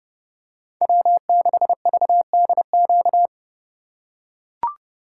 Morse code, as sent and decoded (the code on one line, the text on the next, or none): .-- -.... ...- -.. --.-
W6VDQ